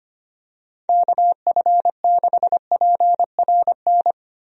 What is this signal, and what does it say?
Morse code, heard as a keyed tone, about 25 wpm, 700 Hz. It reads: KF6PRN
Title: USA Call Signs - 6 Characters - Repeated 2x - 25wpm